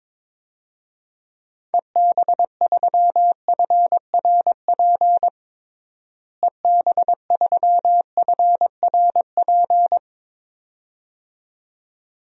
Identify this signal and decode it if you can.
Morse code, heard as a keyed tone, about 22 wpm, 700 Hz. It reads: EB3FRP EB3FRP
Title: DXCC-100 Call Signs - 6 Characters - Repeated 2x - 22wpm